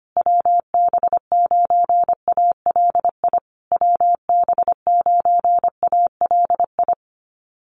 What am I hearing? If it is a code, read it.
Morse code, heard as a keyed tone, about 25 wpm, 700 Hz. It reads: WB9ALI WB9ALI
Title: USA Call Signs Repeated 3x 25wpm